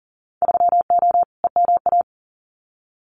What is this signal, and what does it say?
Morse code, heard as a keyed tone, about 40 wpm, 700 Hz. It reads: 3O EMA
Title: Winter Field Day Exchanges 40wpm